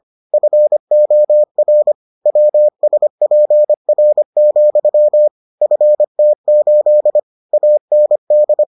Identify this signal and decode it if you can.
Morse code, heard as a keyed tone, about 25 wpm, 600 Hz. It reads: FOR WSPR, FT8 AND